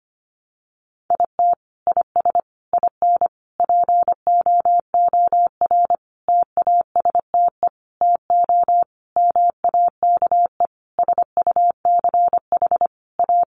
Morse code, heard as a keyed tone, about 25 wpm, 700 Hz. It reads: IT IS IN POOR TASTE TO MAKE SUCH A
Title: Sentences from Top 1200 Words 25wpm